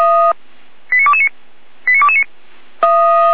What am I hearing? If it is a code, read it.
Five-tone selective call (ZVEI), heard as quick runs of five tones, tones of about 70 ms. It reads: 89229 89229